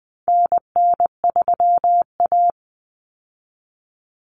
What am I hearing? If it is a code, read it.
Morse code, heard as a keyed tone, about 20 wpm, 700 Hz. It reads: NN3A
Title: Common Cut Numbers - 4-Digit 20wpm